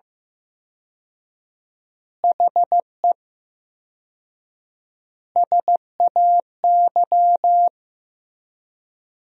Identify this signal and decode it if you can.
Morse code, heard as a keyed tone, about 15 wpm, 700 Hz. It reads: HE SAY
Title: 4x Word-Spacing - Sentences from Top 300 Words - 15wpm